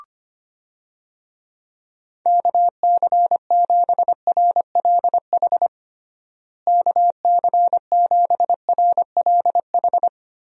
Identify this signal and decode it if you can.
Morse code, heard as a keyed tone, about 25 wpm, 700 Hz. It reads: KC7RLH KC7RLH
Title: USA Call Signs - 6 Characters - Repeated 3x - 25wpm